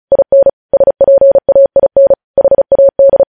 Morse code, heard as a keyed tone, about 36 wpm, 550 Hz. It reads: IN SPAIN HAD
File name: podcast / News headlines-1x-35x35